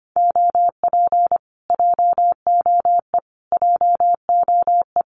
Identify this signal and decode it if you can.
Morse code, heard as a keyed tone, about 25 wpm, 700 Hz. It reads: OP JOE JOE
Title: Common QSO Phrases Encore 25wpm